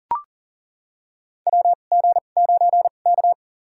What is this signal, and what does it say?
Morse code, heard as a keyed tone, about 40 wpm, spaced as for 29 wpm, 700 Hz. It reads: WG9K
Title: USA Call Signs - 4 Characters - 2x Character Spacing - 40wpm